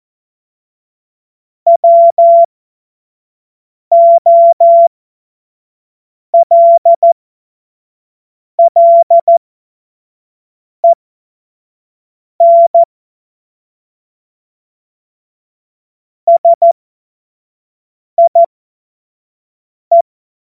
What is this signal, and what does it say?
Morse code, heard as a keyed tone, about 14 wpm, spaced as for 5 wpm, 680 Hz. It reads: WOLLEN SIE
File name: Morsekurs / Witze / Witz 22. Juni 2021, 25 F70